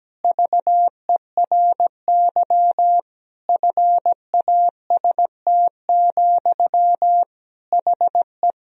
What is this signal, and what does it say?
Morse code, heard as a keyed tone, about 17 wpm, 700 Hz. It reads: VERY FAST, HE
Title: Sentences from Top 1000 Words - Review Part 2 - 17wpm